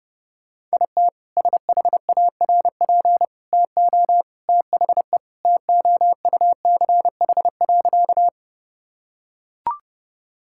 Morse code, heard as a keyed tone, about 30 wpm, 700 Hz. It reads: IT SHARP TO THE TOUCH.
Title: Sentences from Top 1600 Words 30wpm